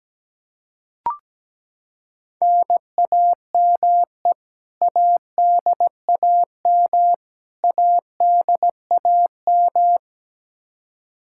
Morse code, heard as a keyed tone, about 17 wpm, 700 Hz. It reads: NAME ADAM ADAM
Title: K1USN SST QSO Practice 17wpm